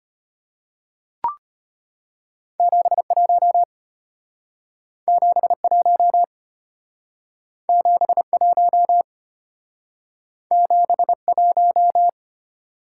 Morse code, heard as a keyed tone, about 32 wpm, 700 Hz. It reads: 71 71 71 71